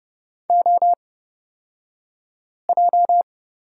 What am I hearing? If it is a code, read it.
Morse code, heard as a keyed tone, about 30 wpm, spaced as for 5 wpm, 700 Hz. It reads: OJ
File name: Lesson 37d - Letter x - Two Characters 30wpm